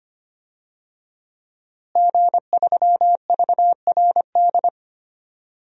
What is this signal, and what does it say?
Morse code, heard as a keyed tone, about 25 wpm, 700 Hz. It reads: G3VRD
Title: DXCC-100 Call Signs - 5 Characters - Repeated 2x - 25wpm